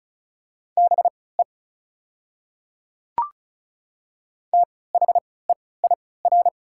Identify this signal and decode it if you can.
Morse code, heard as a keyed tone, about 36 wpm, spaced as for 20 wpm, 700 Hz. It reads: BE THEIR
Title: 3x Character Spacing - Top 100 Words - 35wpm